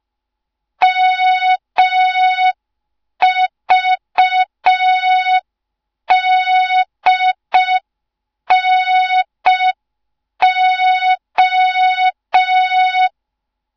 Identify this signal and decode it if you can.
Morse code, heard as a keyed tone, about 5 wpm, 3700 Hz. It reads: MVDNO
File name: Morze-HA2MN / hang / 07 GYAKORLAT